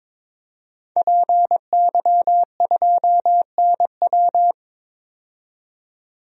Morse code, heard as a keyed tone, about 22 wpm, 700 Hz. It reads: PY2NW
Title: DXCC-100 Call Signs - 5 Characters - 22wpm